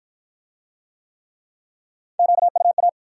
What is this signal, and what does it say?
Morse code, heard as a keyed tone, about 53 wpm, 700 Hz. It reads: QUA